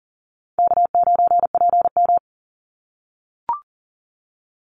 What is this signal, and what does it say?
Morse code, heard as a keyed tone, about 40 wpm, 700 Hz. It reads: K9PM
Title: NA SOTA Activator Callsigns 40wpm